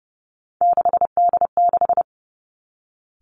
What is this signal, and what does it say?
Morse code, heard as a keyed tone, about 30 wpm, 700 Hz. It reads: 6D6
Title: Three Challenging Letter-Number Combinations 30wpm